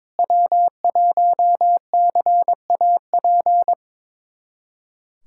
Morse code, heard as a keyed tone, about 22 wpm, 700 Hz. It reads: W1CAP
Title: USA Callsign DE Callsign - 5 Characters - 22wpm